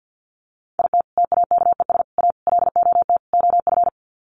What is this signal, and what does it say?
Morse code, heard as a keyed tone, about 50 wpm, 700 Hz. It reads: IT TAKES A LOT OF
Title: Sets of 5 Words Encore 50wpm